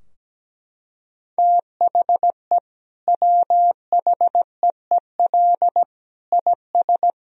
Morse code, heard as a keyed tone, about 17 wpm, 700 Hz. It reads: THE WHEEL IS
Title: Sentences from Top 1000 Words - Review Part 3 - 17wpm